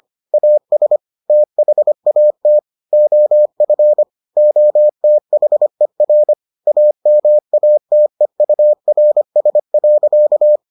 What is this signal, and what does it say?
Morse code, heard as a keyed tone, about 25 wpm, 600 Hz. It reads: AS THAT OF OTHER AMATEURS.